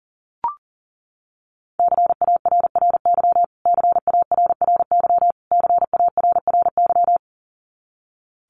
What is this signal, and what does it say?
Morse code, heard as a keyed tone, about 40 wpm, 700 Hz. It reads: CARRY CARRY CARRY
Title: Top 300 Words Repeated 40wpm